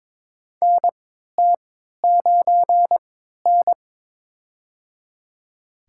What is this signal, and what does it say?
Morse code, heard as a keyed tone, about 22 wpm, spaced as for 12 wpm, 700 Hz. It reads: NT9N
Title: USA Call Signs - 4 Characters - 3x Character Spacing - 22wpm